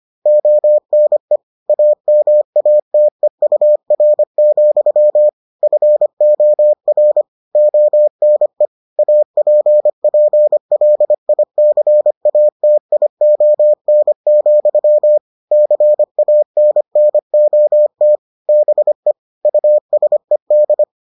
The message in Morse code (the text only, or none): ONE AMATEUR, FOR ONE APPLICATION, CANNOT BE USED